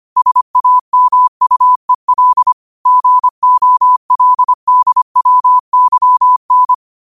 Morse code, heard as a keyed tone, about 25 wpm, 1000 Hz. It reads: SAMUEL GOLDWYN